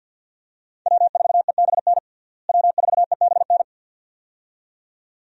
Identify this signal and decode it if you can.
Morse code, heard as a keyed tone, about 53 wpm, 700 Hz. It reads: W4EBN W4EBN
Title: USA Call Signs - 5 Characters - Repeated 2x - 50wpm